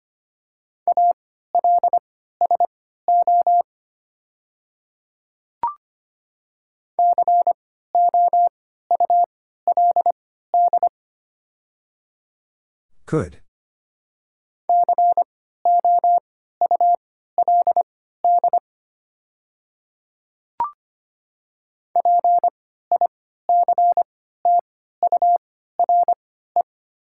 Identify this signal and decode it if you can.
Morse code, heard as a keyed tone, about 25 wpm, spaced as for 14 wpm, 700 Hz. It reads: ALSO COULD COULD PICTURE
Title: 3x Character Spacing - Top 200 Words - 25wpm